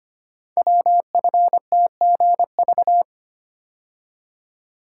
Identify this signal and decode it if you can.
Morse code, heard as a keyed tone, about 25 wpm, 700 Hz. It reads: WFTGV
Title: Code Groups - 5 Letters - 25wpm